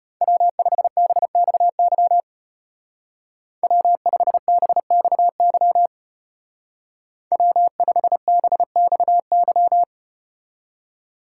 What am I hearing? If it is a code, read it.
Morse code, heard as a keyed tone, about 34 wpm, 700 Hz. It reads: W5BXY W5BXY W5BXY